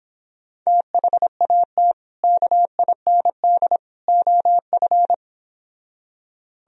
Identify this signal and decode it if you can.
Morse code, heard as a keyed tone, about 26 wpm, 700 Hz. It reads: THAT KIND OF